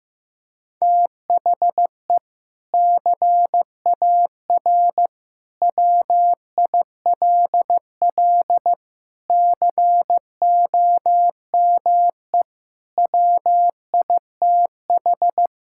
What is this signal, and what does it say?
Morse code, heard as a keyed tone, about 15 wpm, 700 Hz. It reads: THE CAR WILL COME WITH
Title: Sentences from Top 1600 Words 15wpm